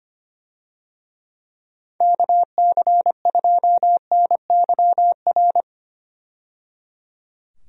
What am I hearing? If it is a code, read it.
Morse code, heard as a keyed tone, about 25 wpm, 700 Hz. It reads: KC2NYR
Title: USA Call Signs - 6 Characters - 25wpm